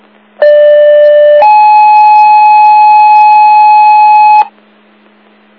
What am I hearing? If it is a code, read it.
Two-tone page, heard as a one-second tone, then a three-second tone, about 590 then 850 Hz. It is heard from 0.4 to 4.4 s.
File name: Old Forge Group